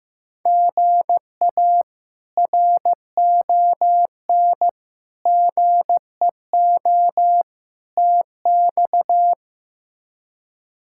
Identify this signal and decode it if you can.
Morse code, heard as a keyed tone, about 15 wpm, 700 Hz. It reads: GA RON GEO TX
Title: K1USN SST QSO Practice - 15wpm